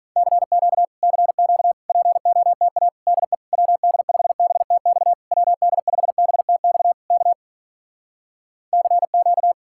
Morse code, heard as a keyed tone, about 48 wpm, 700 Hz. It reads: CQ CQ POTA DE WD5BTX WD5BTX K CQ